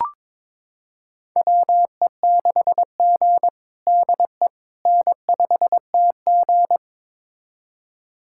Morse code, heard as a keyed tone, about 22 wpm, 700 Hz. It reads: WE6G DE N5TG
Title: USA Callsign DE Callsign - 4 Characters - 22wpm